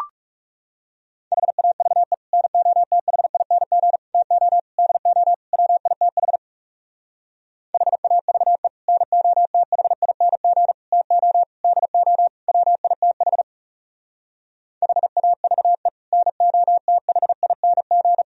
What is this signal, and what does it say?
Morse code, heard as a keyed tone, about 38 wpm, 700 Hz. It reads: HAVE NOTHING TO DO WITH HAVE NOTHING TO DO WITH HAVE NOTHING